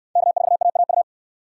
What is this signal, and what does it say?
Morse code, heard as a keyed tone, about 63 wpm, 700 Hz. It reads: K4AAU